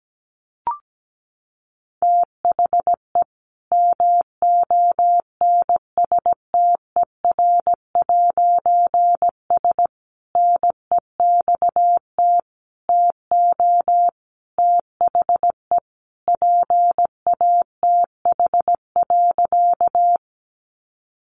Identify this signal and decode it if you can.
Morse code, heard as a keyed tone, about 17 wpm, 700 Hz. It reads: THE MONSTER'S NEXT TO THE PATH.
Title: A Sound Of Thunder 17wpm